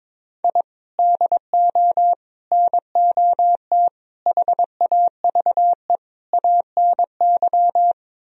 Morse code, heard as a keyed tone, about 22 wpm, 700 Hz. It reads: I DO NOT HAVE ANY
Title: Speed-Racing - Sets of 5 Words 22wpm